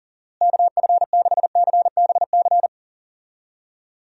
Morse code, heard as a keyed tone, about 37 wpm, 700 Hz. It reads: KF6CBC
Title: USA Call Signs - 6 Characters - Repeated 2x - 40wpm